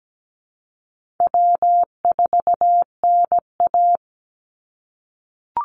Morse code, heard as a keyed tone, about 17 wpm, 700 Hz. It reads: W4NA
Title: Easy Call Signs - 17wpm